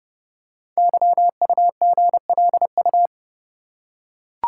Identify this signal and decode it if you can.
Morse code, heard as a keyed tone, about 30 wpm, 700 Hz. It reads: YUGLU
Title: Code Groups - 5 Letter:Numbers - 30wpm